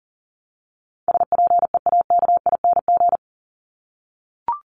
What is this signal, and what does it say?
Morse code, heard as a keyed tone, about 40 wpm, 700 Hz. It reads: SPEAKING
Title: ING Suffix - Top 100 Words with Base Words 40wpm